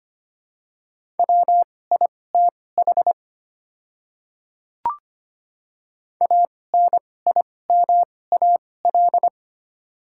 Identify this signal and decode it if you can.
Morse code, heard as a keyed tone, about 25 wpm, spaced as for 18 wpm, 700 Hz. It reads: WITH ANIMAL
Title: 2x Character Spacing - Top 200 Words - 25wpm